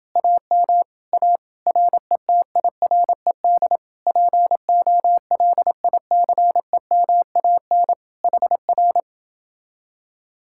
Morse code, heard as a keyed tone, about 29 wpm, 700 Hz. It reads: AM A RETIRED POLICEMAN HR